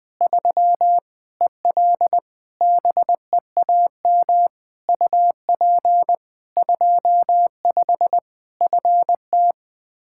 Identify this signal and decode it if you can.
Morse code, heard as a keyed tone, about 20 wpm, 700 Hz. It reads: 3 EL BEAM UP 25 FT
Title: Practice QSO - Exchange 2 - Info - 20wpm